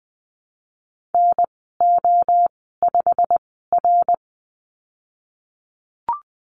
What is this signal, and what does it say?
Morse code, heard as a keyed tone, about 20 wpm, spaced as for 15 wpm, 700 Hz. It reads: NO5R
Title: 088 Lesson 15g - Number 5 - Call Signs 20wpm